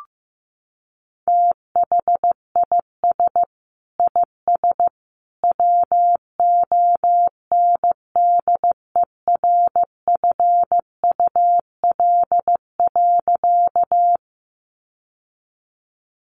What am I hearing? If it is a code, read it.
Morse code, heard as a keyed tone, about 15 wpm, 700 Hz. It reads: THIS IS WONDERFUL.
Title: Sentences from Top 1100 Words 15wpm